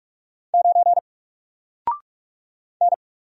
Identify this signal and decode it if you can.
Morse code, heard as a keyed tone, about 44 wpm, 700 Hz. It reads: 9 N
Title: Single Letter-Number Alt 45wpm